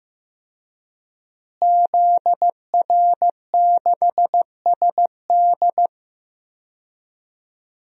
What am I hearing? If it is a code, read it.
Morse code, heard as a keyed tone, about 15 wpm, 700 Hz. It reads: ZR6SD
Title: DXCC-100 Call Signs - 5 Characters - Repeated 3x - 15wpm